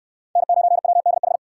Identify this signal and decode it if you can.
Morse code, heard as a keyed tone, about 67 wpm, 700 Hz. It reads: N9WDH